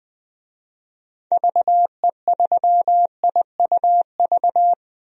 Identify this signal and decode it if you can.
Morse code, heard as a keyed tone, about 20 wpm, 700 Hz. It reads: VE3IUV
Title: DXCC-100 Call Signs - 6 Characters - 20wpm